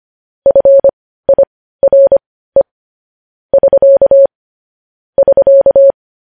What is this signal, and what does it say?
Morse code, heard as a keyed tone, about 25 wpm, spaced as for 15 wpm, 550 Hz. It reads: FIRE <SK> <SK>
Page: news-headlines-in-morse-code-at-15-wpm